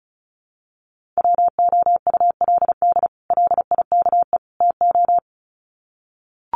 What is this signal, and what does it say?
Morse code, heard as a keyed tone, about 34 wpm, 700 Hz. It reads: WOULD LIKE TO
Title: Sets of 3 Words 35wpm